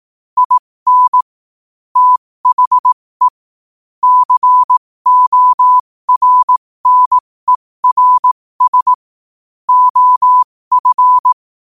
Morse code, heard as a keyed tone, about 18 wpm, spaced as for 15 wpm, 1000 Hz. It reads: IN THE CORNERS OF
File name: qotdfiles / 15wpm